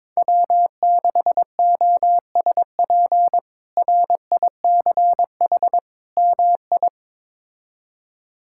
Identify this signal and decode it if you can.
Morse code, heard as a keyed tone, about 21 wpm, 700 Hz. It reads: W6OSP RICH MI
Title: K1USN SST QSO Practice - 22wpm